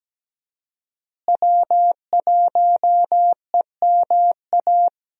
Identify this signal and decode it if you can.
Morse code, heard as a keyed tone, about 17 wpm, 700 Hz. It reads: W1EMA
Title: USA Call Signs - 5 Characters - Repeated 3x - 17wpm